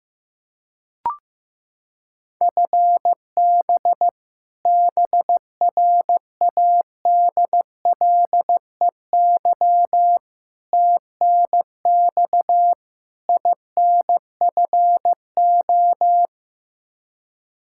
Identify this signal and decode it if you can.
Morse code, heard as a keyed tone, about 16 wpm, 700 Hz. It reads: FB BRADLEY TNX INFO
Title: Practice QSO - Exchange 3 - More Info - 15wpm